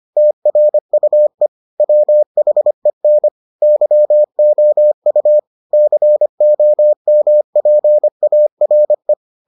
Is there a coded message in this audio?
Yes: Morse code from 0.2 to 9.1 s, heard as a keyed tone, about 25 wpm, 600 Hz. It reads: TRUE WHEN YOU COMPARE